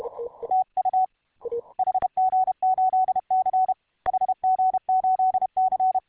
Morse code, heard as a keyed tone, about 31 wpm, 750 Hz. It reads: TU HG8C HG8C